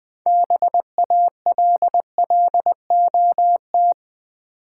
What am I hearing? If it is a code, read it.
Morse code, heard as a keyed tone, about 20 wpm, 700 Hz. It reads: BALLOT
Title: Build A Word - 2 Letters At A Time - 6-Letter Words - 20wpm